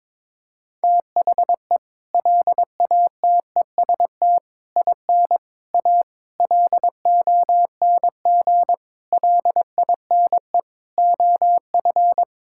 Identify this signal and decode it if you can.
Morse code, heard as a keyed tone, about 22 wpm, 700 Hz. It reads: THE LATEST IN A LONG LINE OF
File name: practice / 22wpm / Sets of 7 Words - 22wpm